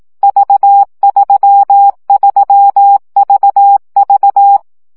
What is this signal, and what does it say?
Morse code, heard as a keyed tone, about 18 wpm, 800 Hz. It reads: V33VV
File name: CW gyakorlas v 3 15 baud